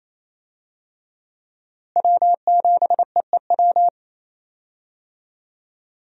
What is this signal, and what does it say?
Morse code, heard as a keyed tone, about 28 wpm, 700 Hz. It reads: W7EEW
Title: USA Call Signs - 5 Characters - Repeated 2x - 28wpm